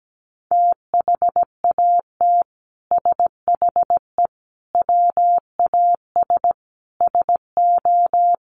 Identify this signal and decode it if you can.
Morse code, heard as a keyed tone, about 17 wpm, 700 Hz. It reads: THAT SHE WAS SO
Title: Sentences from Top 1400 Words 17wpm